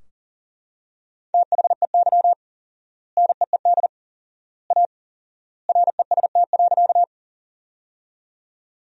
Morse code, heard as a keyed tone, about 37 wpm, 700 Hz. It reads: THEY NEED A REST.
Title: 4x Word-Spacing - Sentences from Top 500 Words - 40wpm